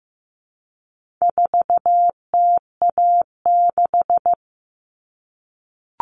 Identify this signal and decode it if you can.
Morse code, heard as a keyed tone, about 15 wpm, 700 Hz. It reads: 4TA6
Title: All Cut Numbers - 4-Digit 15wpm